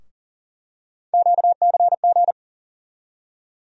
Morse code, heard as a keyed tone, about 40 wpm, 700 Hz. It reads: QCG
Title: USA Call Sign Postfixes - 3 Letters 40wpm